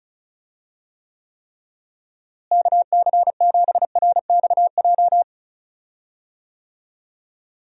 Morse code, heard as a keyed tone, about 34 wpm, 700 Hz. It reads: KC7RXJ
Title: USA Call Signs - 6 Characters - Repeated 3x - 35wpm